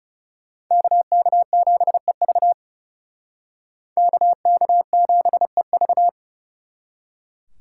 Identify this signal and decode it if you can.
Morse code, heard as a keyed tone, about 37 wpm, 700 Hz. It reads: KK7EV KK7EV
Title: Speed Racing - USA 2x2 Call Signs 30wpm